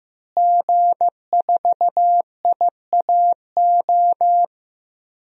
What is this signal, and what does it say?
Morse code, heard as a keyed tone, about 15 wpm, 700 Hz. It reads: G4IAO
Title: DXCC-100 Call Signs - 5 Characters - Repeated 2x - 15wpm